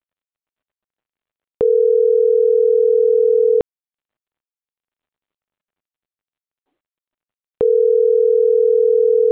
Ringback tone (North America), 440 + 480 Hz.